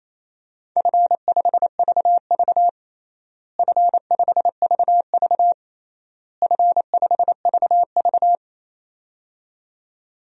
Morse code, heard as a keyed tone, about 28 wpm, 700 Hz. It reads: F5VV F5VV F5VV
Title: DXCC-100 Call Signs - 4 Characters - Repeated 3x - 28wpm